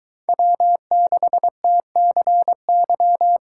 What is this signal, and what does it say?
Morse code, heard as a keyed tone, about 23 wpm, 700 Hz. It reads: W6TCY